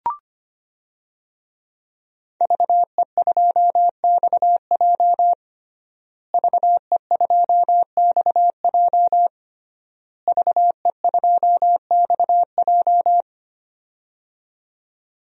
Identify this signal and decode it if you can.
Morse code, heard as a keyed tone, about 25 wpm, 700 Hz. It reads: VE2XJ VE2XJ VE2XJ
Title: DXCC-100 Call Signs - 5 Characters - Repeated 3x - 25wpm